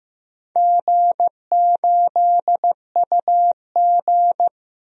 Morse code, heard as a keyed tone, about 15 wpm, 700 Hz. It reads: G8UG
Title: DXCC-100 Call Signs - 4 Characters - Repeated 2x - 15wpm